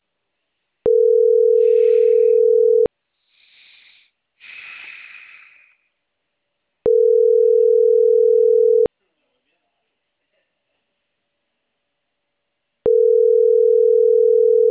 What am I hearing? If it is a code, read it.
Ringback tone (North America), 440 + 480 Hz, with speech.
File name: CENTRAL CAEA / out agcaea01